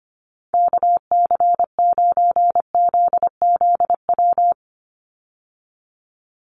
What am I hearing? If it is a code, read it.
Morse code, heard as a keyed tone, about 25 wpm, 700 Hz. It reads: KC9ZZW
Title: USA Call Signs 25wpm